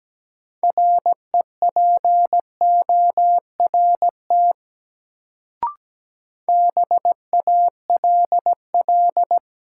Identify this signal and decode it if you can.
Morse code, heard as a keyed tone, about 17 wpm, 700 Hz. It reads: REPORT BALL
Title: Sentences from Top 900 Words - New Word Drill - 17wpm